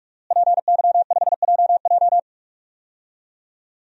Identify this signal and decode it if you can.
Morse code, heard as a keyed tone, about 46 wpm, 700 Hz. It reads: PY5JJ
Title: DXCC-100 Call Signs - 5 Characters - Repeated 2x - 45wpm